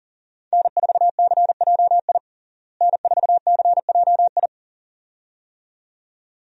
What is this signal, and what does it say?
Morse code, heard as a keyed tone, about 40 wpm, 700 Hz. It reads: N4CJI N4CJI
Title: USA Call Signs - 5 Characters - Repeated 2x - 40wpm